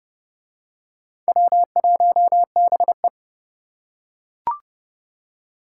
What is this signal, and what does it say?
Morse code, heard as a keyed tone, about 30 wpm, 700 Hz. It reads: W1BE